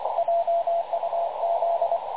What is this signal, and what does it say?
Morse code, heard as a keyed tone, about 25 wpm, 680 Hz, with radio noise. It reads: JUZ
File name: dlazaawansowanych / nagraniazpasma / websdr recording 3567.0kHz